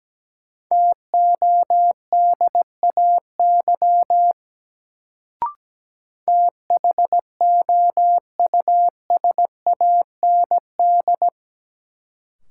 Morse code, heard as a keyed tone, about 17 wpm, 700 Hz. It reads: TODAY THOUSAND
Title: Sentences from Top 600 Words - New Word Drill - 17wpm